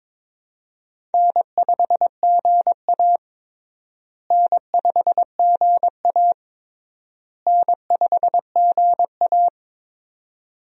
Morse code, heard as a keyed tone, about 22 wpm, 700 Hz. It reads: N5GA N5GA N5GA
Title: USA Call Signs - 4 Characters - Repeated 3x - 22wpm